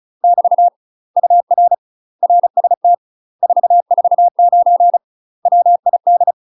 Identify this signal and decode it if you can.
Morse code, heard as a keyed tone, about 36 wpm, 700 Hz. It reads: <BT> UR RST 449 WID
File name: UK MASTER QSOs TRAINING / 35WPM / UK MASTER QSOs 0014